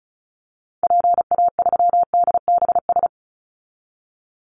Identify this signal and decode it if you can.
Morse code, heard as a keyed tone, about 34 wpm, 700 Hz. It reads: PA3DBS
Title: Call Signs Repeated 3x - 35wpm